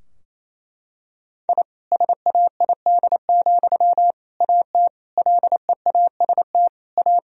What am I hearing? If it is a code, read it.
Morse code, heard as a keyed tone, about 28 wpm, 700 Hz. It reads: I SAID, AT LEAST A
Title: Sentences from Top 1000 Words - Review Part 2 - 28wpm